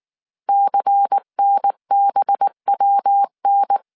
Morse code, heard as a keyed tone, about 19 wpm, 800 Hz. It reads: CNBWN
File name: CW / DET / 1 6 1500C S C 18wpm 2356